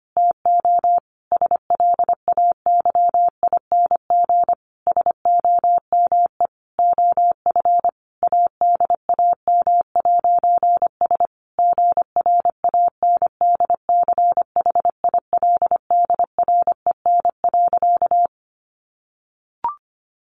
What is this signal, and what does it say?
Morse code, heard as a keyed tone, about 25 wpm, 700 Hz. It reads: TO SLAYING SOME OF ADAM'S GRANDCHILDREN.
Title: A Sound Of Thunder 25wpm